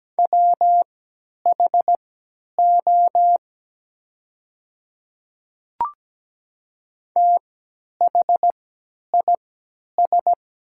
Morse code, heard as a keyed tone, about 17 wpm, spaced as for 10 wpm, 700 Hz. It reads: WHO THIS
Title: 3x Character Spacing - Top 200 Words - 17wpm